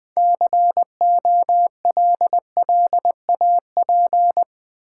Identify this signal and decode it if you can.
Morse code, heard as a keyed tone, about 20 wpm, 700 Hz. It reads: COLLAP
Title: Build A Word - 2 Letters At A Time - 8-Letter Words - 20wpm